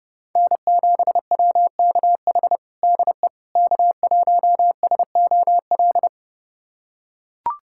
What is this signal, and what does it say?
Morse code, heard as a keyed tone, about 32 wpm, 700 Hz. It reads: N7WKH DE K1SOL
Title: USA Callsign DE Callsign - 5 Characters - 30wpm